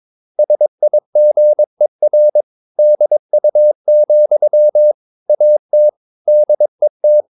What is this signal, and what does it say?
Morse code, heard as a keyed tone, about 22 wpm, 600 Hz. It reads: SIGER DU, AT DET
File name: Ferie QRS 22wpm